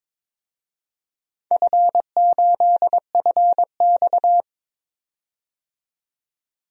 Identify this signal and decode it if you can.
Morse code, heard as a keyed tone, about 22 wpm, 700 Hz. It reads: F8FX
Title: DXCC-100 Call Signs - 4 Characters - 22wpm